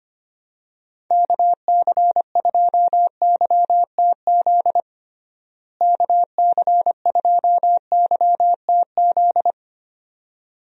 Morse code, heard as a keyed tone, about 25 wpm, 700 Hz. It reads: KC2YTZ KC2YTZ
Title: USA Call Signs - 6 Characters - Repeated 2x - 25wpm